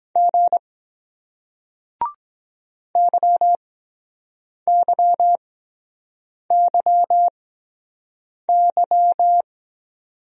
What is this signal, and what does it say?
Morse code, heard as a keyed tone, about 23 wpm, spaced as for 7 wpm, 700 Hz. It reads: G YYYY